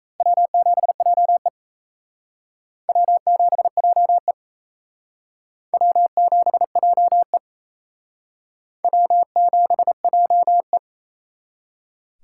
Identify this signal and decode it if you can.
Morse code, heard as a keyed tone, about 34 wpm, 700 Hz. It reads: W7JE W7JE W7JE W7JE
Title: Speed Racing - USA 1x2 Call Signs 28wpm